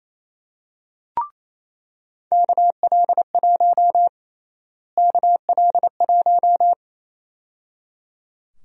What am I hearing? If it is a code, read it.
Morse code, heard as a keyed tone, about 27 wpm, 700 Hz. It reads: KL1 KL1
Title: USA Call Sign Prefixes - Repeated 2x - 28wpm